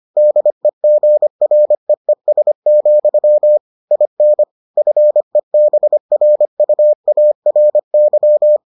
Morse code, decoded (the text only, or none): DEGREES, IN FEBRUARY